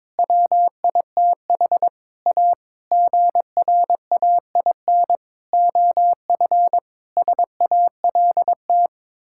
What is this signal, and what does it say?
Morse code, heard as a keyed tone, about 22 wpm, 700 Hz. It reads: WITH A GRAIN OF SALT
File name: practice / 22wpm / Sets of 8 Words - 22wpm